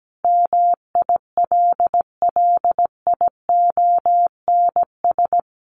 Morse code, heard as a keyed tone, about 17 wpm, 700 Hz. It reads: MILLIONS
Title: Sets of 5 Words 17wpm